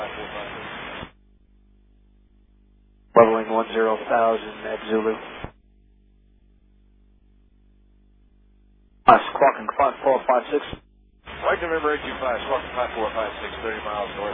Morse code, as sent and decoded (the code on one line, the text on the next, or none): none